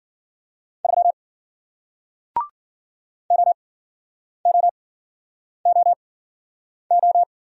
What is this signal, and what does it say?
Morse code, heard as a keyed tone, about 50 wpm, spaced as for 9 wpm, 700 Hz. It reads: 3 OOOO